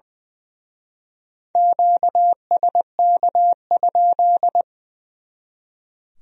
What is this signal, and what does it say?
Morse code, heard as a keyed tone, about 23 wpm, 700 Hz. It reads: QSK?
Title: Speed-Racing Q-Codes - 20wpm